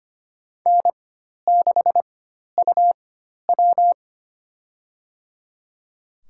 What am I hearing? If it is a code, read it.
Morse code, heard as a keyed tone, about 25 wpm, spaced as for 12 wpm, 700 Hz. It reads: N6UW
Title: USA Call Signs - 4 Characters - 4x Character Spacing - 25wpm